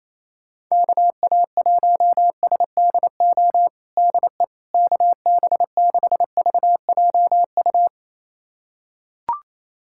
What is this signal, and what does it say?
Morse code, heard as a keyed tone, about 28 wpm, 700 Hz. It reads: KA1SDO DE KB6VJU
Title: USA Callsign DE Callsign - 6 Characters - 28wpm